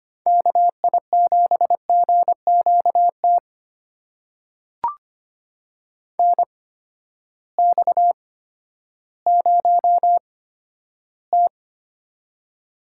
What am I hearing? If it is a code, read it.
Morse code, heard as a keyed tone, about 25 wpm, spaced as for 12 wpm, 700 Hz. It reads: KI7GQT NX0T
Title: USA Call Signs - 8x Character Spacing - 25wpm